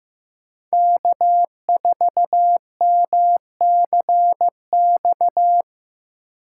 Morse code, heard as a keyed tone, about 18 wpm, 700 Hz. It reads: K4MCX
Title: Speed Racing - USA 1x3 Call Signs 15wpm